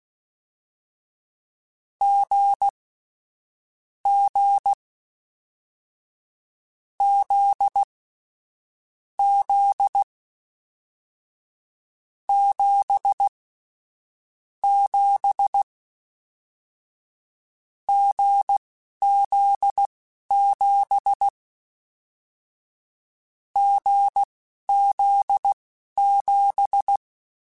Morse code, heard as a keyed tone, about 16 wpm, spaced as for 11 wpm, 800 Hz. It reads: G G Z Z 7 7 GZ7 GZ7
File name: Difficult Combos / Difficult Combos-mp3s / awj1 ndb6 gz7 @10